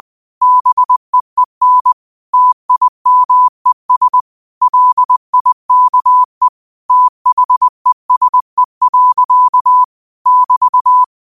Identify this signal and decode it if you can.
Morse code, heard as a keyed tone, about 20 wpm, 1000 Hz. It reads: BEEN TIMES LIKE THESE. <BT>
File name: qotdfiles / 20wpm